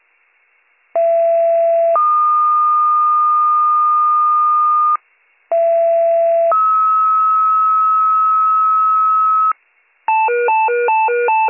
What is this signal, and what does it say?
Two-tone page, heard as a one-second tone, then a three-second tone, about 670 then 1220 Hz, about 670 then 1290 Hz. It is heard from 0.9 to 9.5 s.